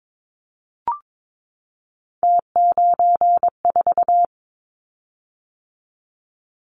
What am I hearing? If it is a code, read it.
Morse code, heard as a keyed tone, about 22 wpm, 700 Hz. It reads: T94
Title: Callsign Prefixes for Top 100 DXCC - 22wpm